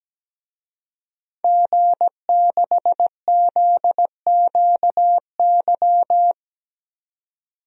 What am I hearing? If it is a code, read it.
Morse code, heard as a keyed tone, about 17 wpm, 700 Hz. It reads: G6ZQY
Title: DXCC-100 Call Signs - 5 Characters - 17wpm